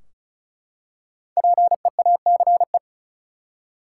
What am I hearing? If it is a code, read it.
Morse code, heard as a keyed tone, about 36 wpm, 700 Hz. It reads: PEACE
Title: Five Letter Words - 35wpm